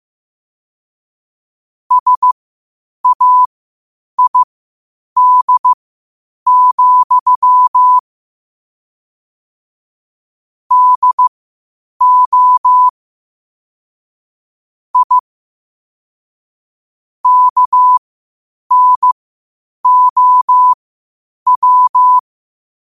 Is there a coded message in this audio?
Yes: Morse code from 1.9 to 22.2 s, heard as a keyed tone, about 15 wpm, spaced as for 8 wpm, 1000 Hz. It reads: SAID, DO I KNOW